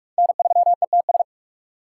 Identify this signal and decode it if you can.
Morse code, heard as a keyed tone, about 43 wpm, 700 Hz. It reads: N3ETS
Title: USA Call Signs - 5 Characters - 2x Character Spacing - 45wpm